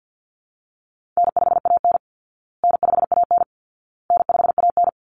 Morse code, heard as a keyed tone, about 50 wpm, 700 Hz. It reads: N5AN N5AN N5AN
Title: Call Signs Repeated 2x - 50wpm